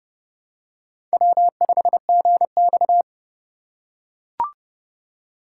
Morse code, heard as a keyed tone, about 31 wpm, 700 Hz. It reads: W5GX
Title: USA Call Signs - 4 Characters - 3x Character Spacing - 30wpm